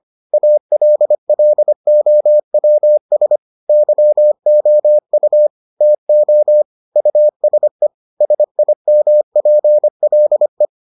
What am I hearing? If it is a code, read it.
Morse code, heard as a keyed tone, about 25 wpm, 600 Hz. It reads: ALLOWS YOU TO USE SIMPLE